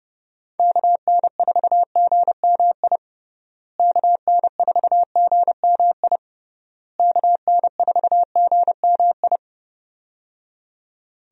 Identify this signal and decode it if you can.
Morse code, heard as a keyed tone, about 30 wpm, 700 Hz. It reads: KN4GMI KN4GMI KN4GMI
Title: USA Call Signs - 6 Characters - Repeated 3x - 30wpm